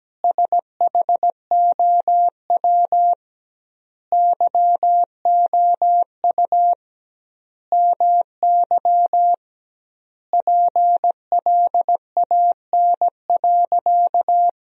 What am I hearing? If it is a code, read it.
Morse code, heard as a keyed tone, about 17 wpm, 700 Hz. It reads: SHOW YOU MY PLAN.
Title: 2x Word-Spacing - Sentences from Top 500 Words - 17wpm